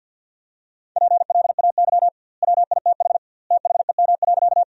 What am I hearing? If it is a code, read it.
Morse code, heard as a keyed tone, about 50 wpm, 700 Hz. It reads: PLAY WITH THEM.
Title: 2x Word-Spacing - Sentences from Top 200 Words - 50wpm